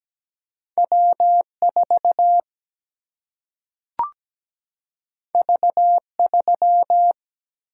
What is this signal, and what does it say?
Morse code, heard as a keyed tone, about 17 wpm, 700 Hz. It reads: W 4 V 3
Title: Single-Letter Single-Number 17wpm